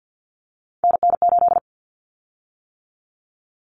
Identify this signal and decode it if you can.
Morse code, heard as a keyed tone, about 50 wpm, 700 Hz. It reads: NN8
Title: Common Cut Numbers - 3-Digit 50wpm